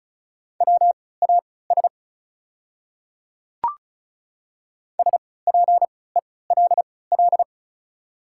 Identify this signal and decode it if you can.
Morse code, heard as a keyed tone, about 34 wpm, spaced as for 20 wpm, 690 Hz. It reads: WAS SPELL
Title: 3x Character Spacing - Top 200 Words - 35wpm